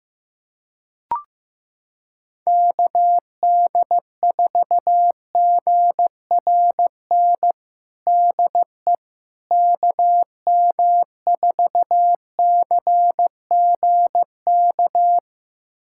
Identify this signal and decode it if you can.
Morse code, heard as a keyed tone, about 15 wpm, 700 Hz. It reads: KD4GRN DE KM4CGK
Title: USA Callsign DE Callsign - 6 Characters - 15wpm